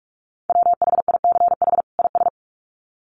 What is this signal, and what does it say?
Morse code, heard as a keyed tone, about 44 wpm, 700 Hz. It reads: WHICH IS
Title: Sets of 2 Words 45wpm